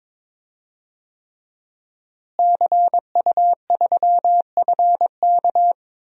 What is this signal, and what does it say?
Morse code, heard as a keyed tone, about 22 wpm, 700 Hz. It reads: CU3FK
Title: DXCC-100 Call Signs - 5 Characters - Repeated 3x - 22wpm